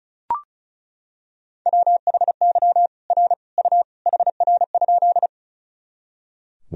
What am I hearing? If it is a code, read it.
Morse code, heard as a keyed tone, about 34 wpm, 690 Hz. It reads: WHY R U HR?
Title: Sentences from Top 200 Words and Abbreviations 35wpm